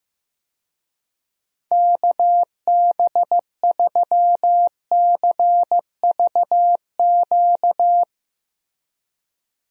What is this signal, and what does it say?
Morse code, heard as a keyed tone, about 15 wpm, 700 Hz. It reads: KB3CVQ
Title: USA Call Signs - 6 Characters - Repeated 3x - 15wpm